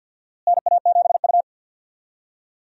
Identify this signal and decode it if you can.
Morse code, heard as a keyed tone, about 52 wpm, 700 Hz. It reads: NA7U
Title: USA Call Signs - 4 Characters - Repeated 2x - 50wpm